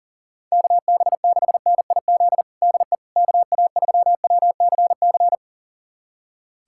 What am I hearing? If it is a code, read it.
Morse code, heard as a keyed tone, about 40 wpm, 700 Hz. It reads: KB6NIZ DE KA3WCC
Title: USA Callsign DE Callsign - 6 Characters - 40wpm